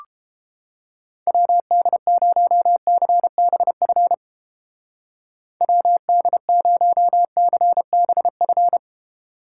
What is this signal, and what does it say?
Morse code, heard as a keyed tone, about 31 wpm, 700 Hz. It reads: WD0CBF WD0CBF